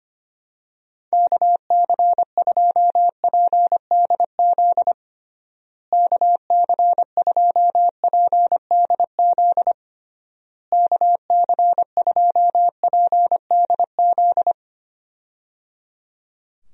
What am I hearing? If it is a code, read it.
Morse code, heard as a keyed tone, about 25 wpm, 700 Hz. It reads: KC2PDZ KC2PDZ KC2PDZ
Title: USA Call Signs - 6 Characters - Repeated 3x - 25wpm